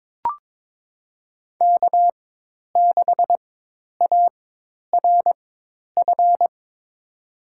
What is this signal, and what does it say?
Morse code, heard as a keyed tone, about 22 wpm, spaced as for 10 wpm, 700 Hz. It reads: K6ARF
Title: USA Call Signs - 5 Characters - 4x Character Spacing - 22wpm